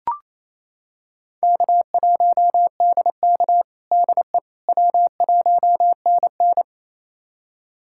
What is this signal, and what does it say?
Morse code, heard as a keyed tone, about 28 wpm, 700 Hz. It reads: K1DK DE W1NN
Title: USA Callsign DE Callsign - 4 Characters - 28wpm